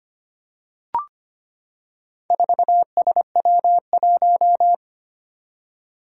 Morse code, heard as a keyed tone, about 25 wpm, 700 Hz. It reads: 4SW1
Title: Four Challenging Letter-Number Combinations 25wpm